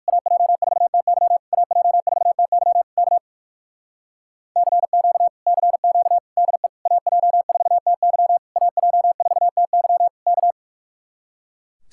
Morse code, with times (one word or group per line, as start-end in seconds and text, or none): 0.1-1.4 AJ4TY
1.5-2.8 AJ4TY
3.0-3.2 K
4.6-5.3 CQ
5.5-6.2 CQ
6.4-6.7 DE
6.9-8.4 AJ4TY
8.6-10.1 AJ4TY
10.3-10.5 K